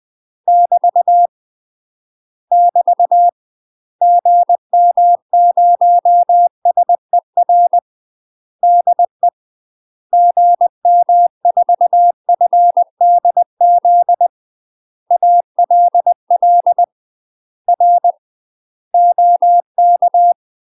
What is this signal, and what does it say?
Morse code, heard as a keyed tone, about 20 wpm, 700 Hz. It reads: <BT> <BT> GM0SER DE GM4FDZ ALL R OK
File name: UK MASTER QSOs TRAINING / 20WPM / UK MASTER QSOs 0067